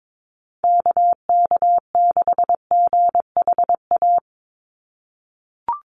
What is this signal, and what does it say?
Morse code, heard as a keyed tone, about 22 wpm, 700 Hz. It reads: KK6GHA
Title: Call Signs - 22wpm